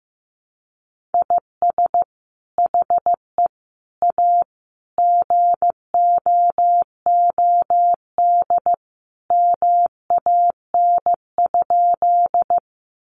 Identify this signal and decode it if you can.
Morse code, heard as a keyed tone, about 15 wpm, 700 Hz. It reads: IS HE A GOOD MAN?
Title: Sentences from Top 200 Words 15wpm